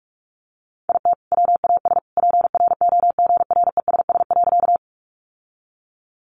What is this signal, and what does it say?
Morse code, heard as a keyed tone, about 44 wpm, 700 Hz. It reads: IT WAS PROGRESS.
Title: Sentences from Top 1400 Words 45wpm